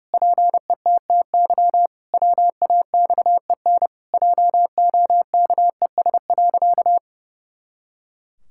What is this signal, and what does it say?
Morse code, heard as a keyed tone, about 30 wpm, 700 Hz. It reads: PETTY WAXEN JOKES.
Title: Pangram Challenge v2 - 30wpm